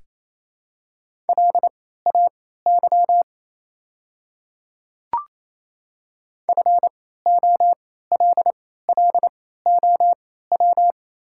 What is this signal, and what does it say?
Morse code, heard as a keyed tone, about 28 wpm, spaced as for 16 wpm, 700 Hz. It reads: LAY FOLLOW
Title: 3x Character Spacing - Top 500 Words - 28wpm